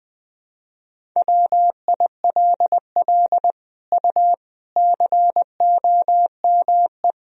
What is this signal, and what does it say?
Morse code, heard as a keyed tone, about 21 wpm, 700 Hz. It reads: WILL U COME
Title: Sentences from Top 100 Words and Abbreviations 20wpm